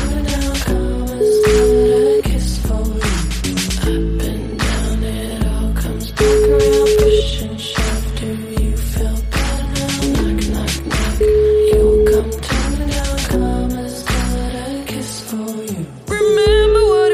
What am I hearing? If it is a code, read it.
Ringback tone (Europe, 425 Hz), with noise.